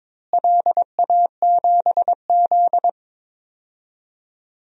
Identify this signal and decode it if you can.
Morse code, heard as a keyed tone, about 21 wpm, 700 Hz. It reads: LA7Z
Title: DXCC-100 Call Signs - 4 Characters - Repeated 3x - 22wpm